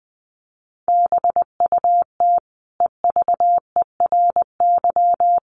Morse code, heard as a keyed tone, about 20 wpm, 700 Hz. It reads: BUT EVERY
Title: Sentences from Top 1400 Words 20wpm